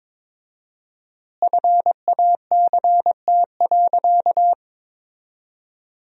Morse code, heard as a keyed tone, about 22 wpm, 700 Hz. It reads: FACT.
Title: 4x Word-Spacing - Sentences from Top 500 Words - 22wpm